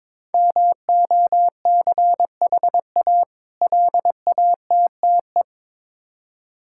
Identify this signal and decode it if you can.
Morse code, heard as a keyed tone, about 23 wpm, 690 Hz. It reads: MOCHA LATTE
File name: Difficult Words with Hints - 22wpm